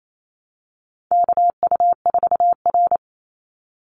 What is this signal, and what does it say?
Morse code, heard as a keyed tone, about 28 wpm, 700 Hz. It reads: KU4R
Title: NA SOTA Chaser Callsigns 28wpm